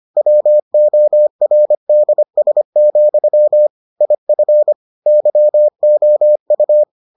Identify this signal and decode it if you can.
Morse code, heard as a keyed tone, about 25 wpm, 600 Hz. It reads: WORDS, IF YOU